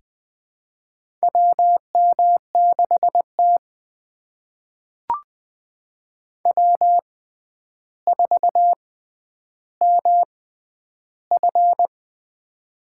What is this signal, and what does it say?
Morse code, heard as a keyed tone, about 20 wpm, spaced as for 10 wpm, 700 Hz. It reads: WM6T W4MF
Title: USA Call Signs - 6x Character Spacing - 20wpm